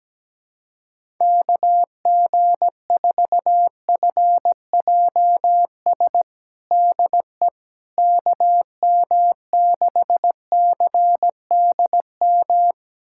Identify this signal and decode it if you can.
Morse code, heard as a keyed tone, about 17 wpm, 700 Hz. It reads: KG4FJS DE KM6CDM
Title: USA Callsign DE Callsign 17wpm